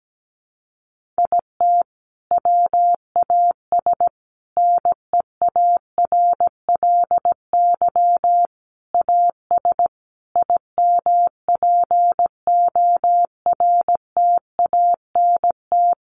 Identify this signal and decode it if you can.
Morse code, heard as a keyed tone, about 17 wpm, 700 Hz. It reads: IT WAS NEARLY AS IMPORTANT
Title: Sentences from Top 1400 Words 17wpm